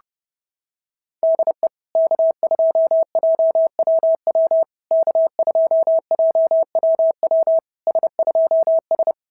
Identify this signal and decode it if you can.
Morse code, heard as a keyed tone, about 30 wpm, 650 Hz. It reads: DE K2JWW K2JWW S2S
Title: SOTA QSO Practice 30wpm